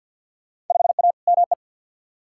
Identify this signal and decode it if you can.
Morse code, heard as a keyed tone, about 48 wpm, 700 Hz. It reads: 5A ME
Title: Field Day Exchanges Encore - 50wpm